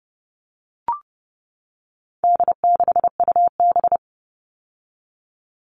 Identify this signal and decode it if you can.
Morse code, heard as a keyed tone, about 30 wpm, 700 Hz. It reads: D6UB
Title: All Cut Numbers - 4-Digit 30wpm